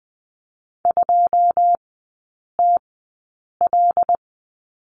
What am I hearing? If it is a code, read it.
Morse code, heard as a keyed tone, about 20 wpm, spaced as for 8 wpm, 700 Hz. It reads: 2TL
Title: 100 Lesson 17e - Letter u - Three Characters 20wpm